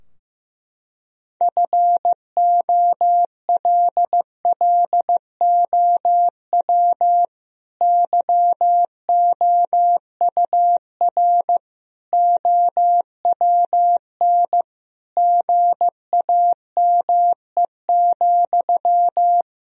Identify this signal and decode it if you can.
Morse code, heard as a keyed tone, about 15 wpm, 700 Hz. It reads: FOLLOW YOUR OWN GAME,
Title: Sentences from Top 1000 Words - Review Part 3 - 15wpm